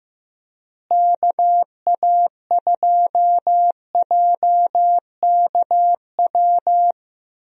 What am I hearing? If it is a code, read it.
Morse code, heard as a keyed tone, about 15 wpm, 700 Hz. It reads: KA2JKW
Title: USA Call Signs - 6 Characters - Repeated 3x - 15wpm